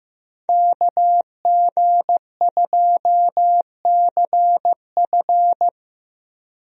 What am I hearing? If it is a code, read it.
Morse code, heard as a keyed tone, about 15 wpm, 700 Hz. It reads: KG2CF
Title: USA Call Signs - 5 Characters - Repeated 3x - 15wpm